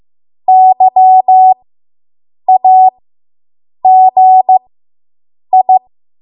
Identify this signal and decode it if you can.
Morse code, heard as a keyed tone, about 15 wpm, spaced as for 7 wpm, 750 Hz. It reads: YAGI